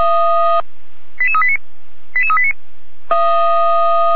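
Five-tone selective call (ZVEI), heard as quick runs of five tones, tones of about 70 ms. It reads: 88389 88389
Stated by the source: ZVEI five-tone selective call received by SDR.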